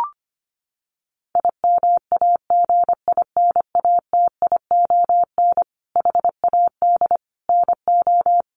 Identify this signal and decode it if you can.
Morse code, heard as a keyed tone, about 25 wpm, 700 Hz. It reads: IMAGINATION HAD NO
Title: Sentences from Top 1300 Words 25wpm